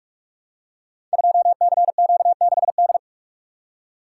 Morse code, heard as a keyed tone, about 44 wpm, 700 Hz. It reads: 2CQ6D
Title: Code Groups - 5 Letter:Numbers - 45wpm